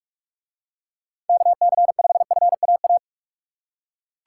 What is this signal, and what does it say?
Morse code, heard as a keyed tone, about 41 wpm, 700 Hz. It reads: KC5FAA